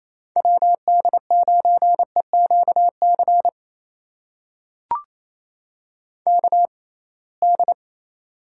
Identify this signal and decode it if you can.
Morse code, heard as a keyed tone, about 28 wpm, spaced as for 21 wpm, 700 Hz. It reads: WD9EQC KD
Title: USA Call Signs - 6 Characters - 6x Character Spacing - 28wpm